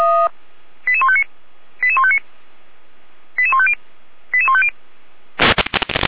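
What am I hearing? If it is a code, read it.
Five-tone selective call (ZVEI), heard as quick runs of five tones, tones of about 70 ms. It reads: 88159 88159 88150 88150
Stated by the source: ZVEI five-tone selective call, received by SDR.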